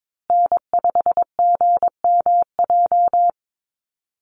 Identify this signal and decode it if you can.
Morse code, heard as a keyed tone, about 22 wpm, 700 Hz. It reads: N5GMJ
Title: USA Call Signs Repeated 2x 22wpm